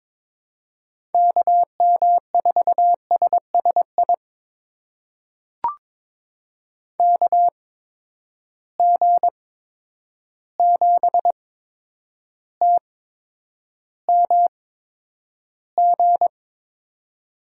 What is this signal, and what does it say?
Morse code, heard as a keyed tone, about 22 wpm, spaced as for 9 wpm, 700 Hz. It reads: KM4SSI KG7TMG
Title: USA Call Signs - 6 Characters - 8x Character Spacing - 22wpm